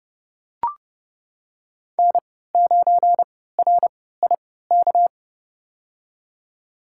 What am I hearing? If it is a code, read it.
Morse code, heard as a keyed tone, about 29 wpm, spaced as for 17 wpm, 700 Hz. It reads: N9RIK
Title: USA Call Signs - 3x Character Spacing - 30wpm